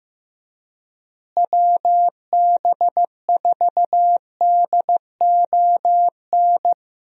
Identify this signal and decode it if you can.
Morse code, heard as a keyed tone, about 15 wpm, 700 Hz. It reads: WB4DON
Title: USA Call Signs - 6 Characters - Repeated 2x - 15wpm